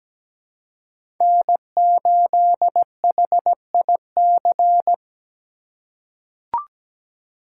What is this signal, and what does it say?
Morse code, heard as a keyed tone, about 17 wpm, 700 Hz. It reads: N8HIC
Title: USA Call Signs - 5 Characters - 2x Character Spacing - 17wpm